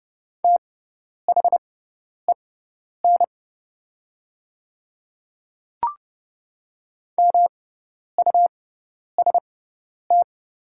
Morse code, heard as a keyed tone, about 30 wpm, spaced as for 10 wpm, 700 Hz. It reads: THEN MUST
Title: 6x Character Spacing - Top 200 Words - 30wpm